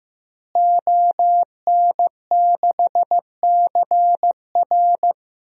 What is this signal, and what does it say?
Morse code, heard as a keyed tone, about 15 wpm, 700 Hz. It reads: ON6CR
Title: DXCC-100 Call Signs - 5 Characters - Repeated 2x - 15wpm